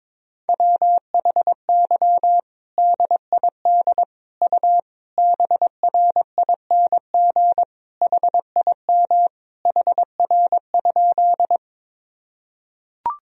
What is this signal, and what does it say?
Morse code, heard as a keyed tone, about 22 wpm, 700 Hz. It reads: WHY DID U BRING HIM HR?
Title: Sentences from Top 500 Words and Abbreviations 22wpm